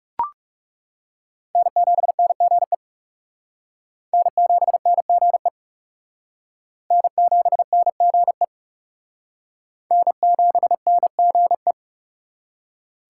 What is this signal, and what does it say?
Morse code, heard as a keyed tone, about 38 wpm, 700 Hz. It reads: N7NGE N7NGE N7NGE N7NGE